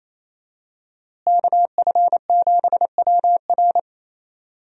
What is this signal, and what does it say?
Morse code, heard as a keyed tone, about 28 wpm, 700 Hz. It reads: KF7WR
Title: USA Call Signs - 5 Characters - Repeated 3x - 28wpm